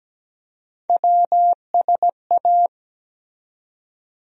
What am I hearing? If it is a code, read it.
Morse code, heard as a keyed tone, about 17 wpm, 700 Hz. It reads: WSA
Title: USA Call Sign Postfixes - 3 Letters 17wpm